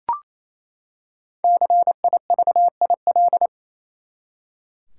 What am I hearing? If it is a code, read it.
Morse code, heard as a keyed tone, about 28 wpm, 700 Hz. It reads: CIVIL
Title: Five Letter Words - 28wpm